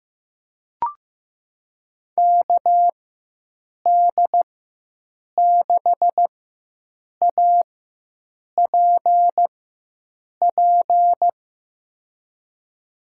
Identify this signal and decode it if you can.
Morse code, heard as a keyed tone, about 15 wpm, spaced as for 7 wpm, 700 Hz. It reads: KD6APP
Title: USA Call Signs - 6 Characters - 4x Character Spacing - 15wpm